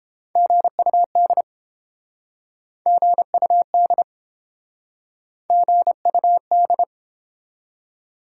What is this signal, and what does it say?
Morse code, heard as a keyed tone, about 30 wpm, 700 Hz. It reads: GUD GUD GUD